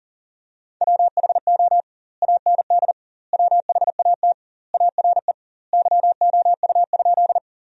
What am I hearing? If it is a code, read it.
Morse code, heard as a keyed tone, about 37 wpm, 700 Hz. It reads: WHO AND WHAT ARE YOU?
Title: 2x Word-Spacing - Sentences from Top 100 Words - 40wpm